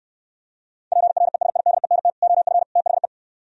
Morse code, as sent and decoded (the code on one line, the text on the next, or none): .--. .-. ... .. -.. -. - / --- ..-. / - .... .
PRESIDENT OF THE